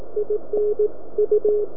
Morse code, heard as a keyed tone, about 18 wpm, 420 Hz. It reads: IN U